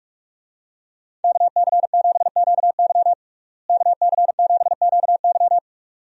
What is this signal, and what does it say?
Morse code, heard as a keyed tone, about 45 wpm, 700 Hz. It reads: KC7QY KC7QY
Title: USA Call Signs - 5 Characters - Repeated 2x - 45wpm